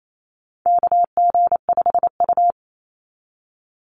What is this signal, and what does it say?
Morse code, heard as a keyed tone, about 28 wpm, 700 Hz. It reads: KG5U
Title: Call Signs Repeated 3x - 28wpm